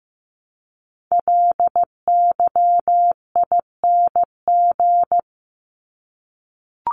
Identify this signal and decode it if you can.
Morse code, heard as a keyed tone, about 15 wpm, 700 Hz. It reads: LYING
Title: ING Suffix - Top 100 Words with Base Words 15wpm